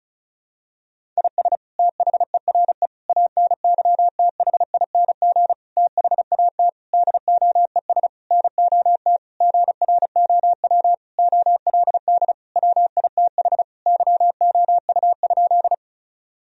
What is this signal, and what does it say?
Morse code, heard as a keyed tone, about 36 wpm, 700 Hz. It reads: IS THERE ANYTHING THAT DOES NOT GROW OLD WITH YOU?
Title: Sentences from Top 1000 Words - Review Part 1 - 35wpm